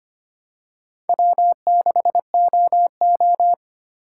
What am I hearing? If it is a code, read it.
Morse code, heard as a keyed tone, about 25 wpm, 700 Hz. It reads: W6OO
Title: USA Call Signs - 4 Characters - 4x Character Spacing - 25wpm